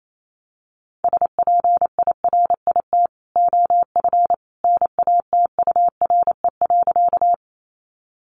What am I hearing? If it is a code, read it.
Morse code, heard as a keyed tone, about 28 wpm, 700 Hz. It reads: SPIRIT OF NATURE.
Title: Sentences from Top 1400 Words 28wpm